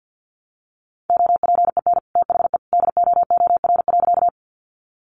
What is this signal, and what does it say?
Morse code, heard as a keyed tone, about 50 wpm, 700 Hz. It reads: OPEN THE DOOR.
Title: Sentences from Top 300 Words 50wpm